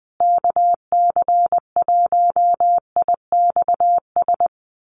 Morse code, heard as a keyed tone, about 20 wpm, 700 Hz. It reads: KC1IXS
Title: USA Call Signs Repeated 3x 20wpm